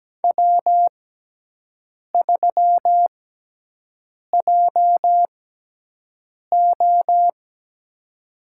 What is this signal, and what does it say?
Morse code, heard as a keyed tone, about 17 wpm, spaced as for 6 wpm, 700 Hz. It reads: W3JO
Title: USA Call Signs - 6x Character Spacing - 17wpm